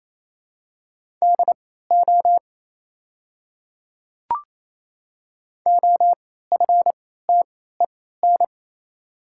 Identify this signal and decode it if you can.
Morse code, heard as a keyed tone, about 27 wpm, spaced as for 16 wpm, 700 Hz. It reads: DO OFTEN
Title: 3x Character Spacing - Top 300 Words - 28wpm